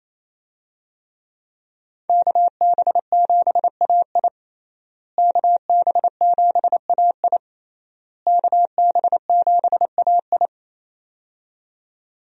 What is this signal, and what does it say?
Morse code, heard as a keyed tone, about 28 wpm, 700 Hz. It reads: KB7AI KB7AI KB7AI
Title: USA Call Signs - 5 Characters - Repeated 3x - 28wpm